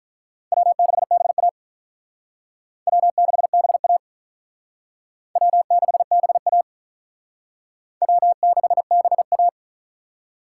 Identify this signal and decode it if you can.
Morse code, heard as a keyed tone, about 45 wpm, 700 Hz. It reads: W6BA W6BA W6BA W6BA